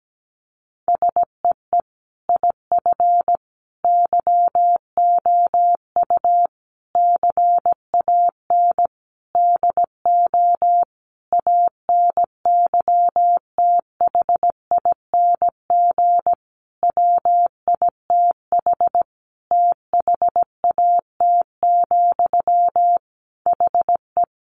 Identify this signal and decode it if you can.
Morse code, heard as a keyed tone, about 17 wpm, 700 Hz. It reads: SEE IF YOU CAN DO ANYTHING WITH THAT, HE
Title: Sentences from Top 1000 Words 17wpm